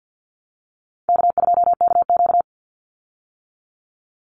Morse code, heard as a keyed tone, about 50 wpm, 700 Hz. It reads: K2KQ
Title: Easy Call Signs - 50wpm